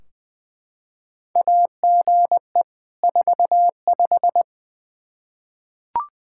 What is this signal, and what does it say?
Morse code, heard as a keyed tone, about 21 wpm, 700 Hz. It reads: AGE 45
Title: Practice QSO - Exchange 3 - More Info - 20wpm